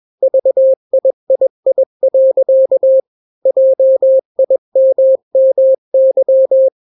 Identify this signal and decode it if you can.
Morse code, heard as a keyed tone, about 21 wpm, 530 Hz. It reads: VIII. JIMMY
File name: daily / 20wpm